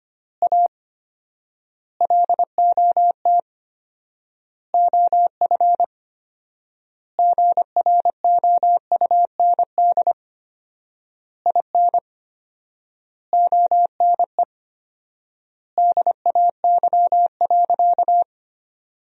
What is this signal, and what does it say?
Morse code, heard as a keyed tone, about 25 wpm, 700 Hz. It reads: A LOT OF GROUND IN ONE DAY.
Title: 4x Word-Spacing - Sentences from Top 500 Words - 25wpm